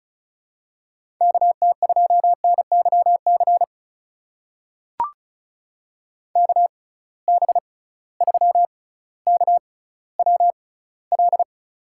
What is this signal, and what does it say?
Morse code, heard as a keyed tone, about 34 wpm, spaced as for 18 wpm, 700 Hz. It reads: KT2NYC KB3KWL
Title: USA Call Signs - 6 Characters - 6x Character Spacing - 35wpm